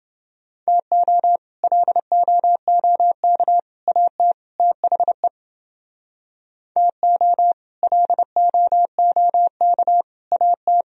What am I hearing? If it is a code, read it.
Morse code, heard as a keyed tone, about 29 wpm, 700 Hz. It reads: TO LOOK AT THE TO LOOK AT